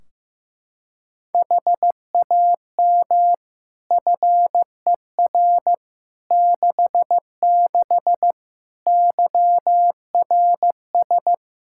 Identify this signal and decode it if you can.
Morse code, heard as a keyed tone, about 15 wpm, 700 Hz. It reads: HAM FER 66 YRS
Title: K1USN SST QSO Practice 15wpm